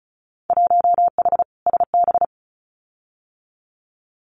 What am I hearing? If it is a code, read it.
Morse code, heard as a keyed tone, about 34 wpm, 700 Hz. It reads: 1H SB
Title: Winter Field Day Exchanges 35wpm